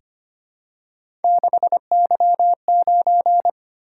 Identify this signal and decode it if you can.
Morse code, heard as a keyed tone, about 25 wpm, 700 Hz. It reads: 6Y9
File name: Three Letter-Number Combinations - 25wpm